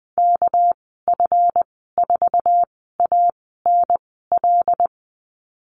Morse code, heard as a keyed tone, about 20 wpm, spaced as for 15 wpm, 700 Hz. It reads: KF4ANL
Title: 238 Lesson 38g - Letter q - Call Signs 20wpm